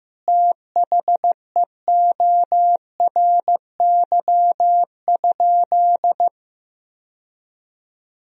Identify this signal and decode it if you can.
Morse code, heard as a keyed tone, about 16 wpm, 700 Hz. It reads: THEORY?
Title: Sentences from Top 1000 Words - Review Part 3 - 15wpm